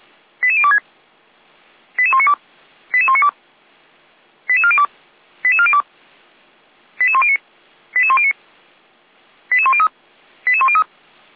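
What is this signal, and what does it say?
Five-tone selective call (ZVEI), heard as quick runs of five tones, tones of about 70 ms. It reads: 88026 88192 88192 88402 88402 88109 88109 88103 88103
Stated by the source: Received on SDR, ZVEI code 88102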